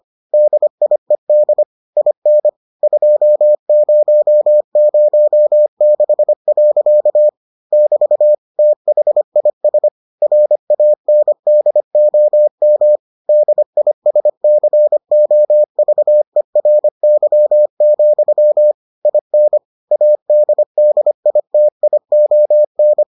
Morse code, heard as a keyed tone, about 25 wpm, 600 Hz. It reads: DIED IN 2006. <BT> THIS RANDOM DISCOVERY, IN ADDITION